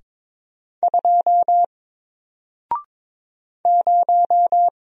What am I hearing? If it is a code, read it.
Morse code, heard as a keyed tone, about 22 wpm, 700 Hz. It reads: 2 0
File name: Single-Letter-Number-22wpm